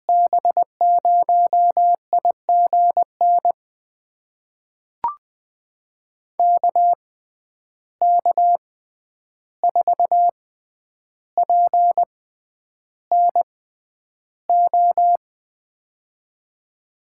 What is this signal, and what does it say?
Morse code, heard as a keyed tone, about 20 wpm, spaced as for 10 wpm, 700 Hz. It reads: B0IGN KK4PNO
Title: USA Call Signs - 6 Characters - 6x Character Spacing - 20wpm